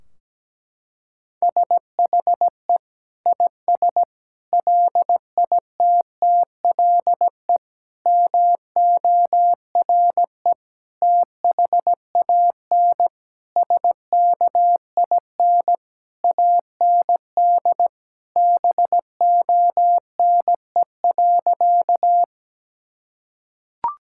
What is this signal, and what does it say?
Morse code, heard as a keyed tone, about 17 wpm, 700 Hz. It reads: SHE IS LITTLE MORE THAN SKIN AND BONE.
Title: Sentences from Top 1600 Words 17wpm